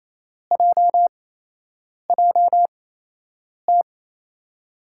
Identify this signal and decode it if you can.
Morse code, heard as a keyed tone, about 28 wpm, spaced as for 8 wpm, 700 Hz. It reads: JJT
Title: USA Call Signs - 6 Characters - 8x Character Spacing - 28wpm